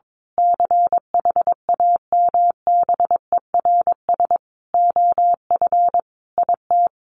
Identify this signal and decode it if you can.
Morse code, heard as a keyed tone, about 22 wpm, 700 Hz. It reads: CHAMBERS OF IT
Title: A Sound Of Thunder 22wpm